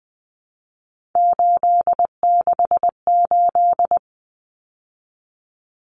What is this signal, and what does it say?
Morse code, heard as a keyed tone, about 20 wpm, 700 Hz. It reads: 868
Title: Numbers - 3-Digit 20wpm